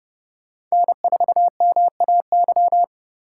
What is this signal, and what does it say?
Morse code, heard as a keyed tone, about 36 wpm, 700 Hz. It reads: N4MAY
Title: Speed Racing - USA 1x3 Call Signs 30wpm